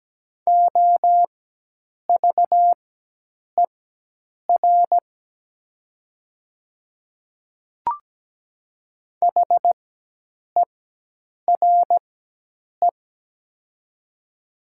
Morse code, heard as a keyed tone, about 17 wpm, spaced as for 8 wpm, 700 Hz. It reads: OVER HERE
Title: 4x Character Spacing - Top 200 Words - 17wpm